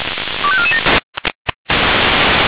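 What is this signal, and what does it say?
Five-tone selective call (ZVEI), heard as quick runs of five tones, tones of about 70 ms. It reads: 25447
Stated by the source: SDR-received ZVEI five-tone call 25431.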